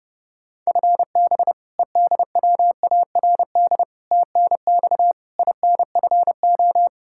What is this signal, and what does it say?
Morse code, heard as a keyed tone, about 30 wpm, 700 Hz. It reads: FB EDWARD TNX INFO
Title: Practice QSO - Exchange 3 - More Info - 30wpm